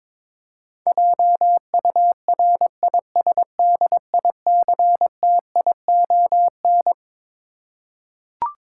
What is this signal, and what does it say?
Morse code, heard as a keyed tone, about 22 wpm, 700 Hz. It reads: JURISDICTION
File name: Twelve Letter Words - 22wpm